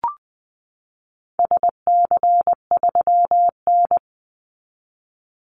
Morse code, heard as a keyed tone, about 20 wpm, 700 Hz. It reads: SC3N
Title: Call Signs - 20wpm